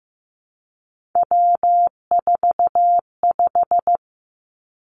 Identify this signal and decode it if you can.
Morse code, heard as a keyed tone, about 15 wpm, 700 Hz. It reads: W45
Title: Three Challenging Letter-Number Combinations 15wpm